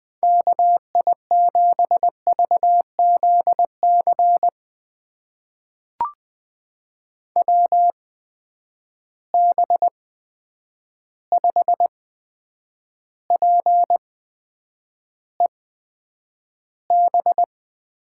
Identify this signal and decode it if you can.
Morse code, heard as a keyed tone, about 20 wpm, spaced as for 9 wpm, 700 Hz. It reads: KI7VZC WB5PEB
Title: USA Call Signs - 6 Characters - 8x Character Spacing - 20wpm